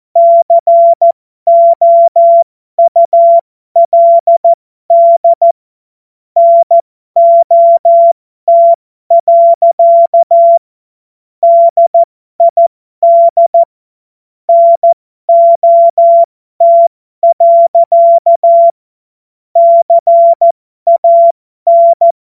Morse code, heard as a keyed tone, about 14 wpm, spaced as for 12 wpm, 680 Hz. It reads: COULD NOT. DID NOT. CAN
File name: Morsekurs / Witze